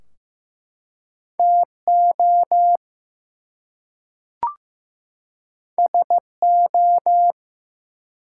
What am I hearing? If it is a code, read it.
Morse code, heard as a keyed tone, about 15 wpm, 700 Hz. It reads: TO SO
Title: Two Letter Words - 15wpm